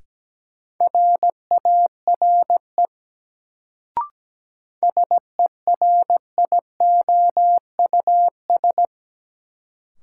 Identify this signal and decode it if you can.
Morse code, heard as a keyed tone, about 17 wpm, 700 Hz. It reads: RARE SERIOUS
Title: Sentences from Top 1100 Words - New Word Drill Alt - 17wpm